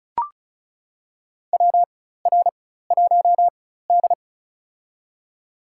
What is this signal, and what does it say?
Morse code, heard as a keyed tone, about 34 wpm, spaced as for 16 wpm, 700 Hz. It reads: WR1D
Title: USA Call Signs - 4x Character Spacing - 35wpm